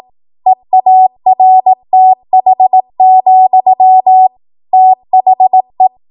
Morse code, heard as a keyed tone, about 18 wpm, 750 Hz. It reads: EARTH, THE
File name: Archive / 18 WPM / 18WPM